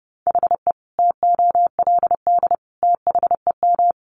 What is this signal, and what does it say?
Morse code, heard as a keyed tone, about 30 wpm, 700 Hz. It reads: HE TOLD THEM
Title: Sentences from Top 1200 Words 30wpm